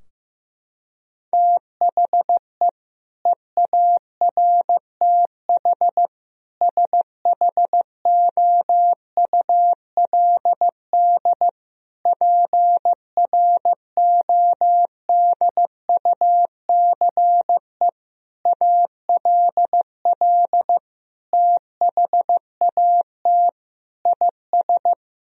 Morse code, heard as a keyed tone, about 15 wpm, 700 Hz. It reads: THE EARTH SHOULD PRODUCE ALL THAT IS
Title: Sentences from Top 1000 Words - Review Part 3 - 15wpm